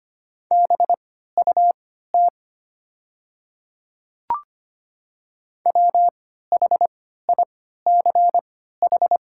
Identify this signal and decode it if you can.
Morse code, heard as a keyed tone, about 25 wpm, spaced as for 14 wpm, 700 Hz. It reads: BUT WHICH
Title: 3x Character Spacing - Top 100 Words - 25wpm